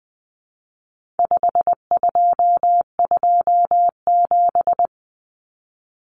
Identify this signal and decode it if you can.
Morse code, heard as a keyed tone, about 20 wpm, 700 Hz. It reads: 5227
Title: Common Cut Numbers - 4-Digit 20wpm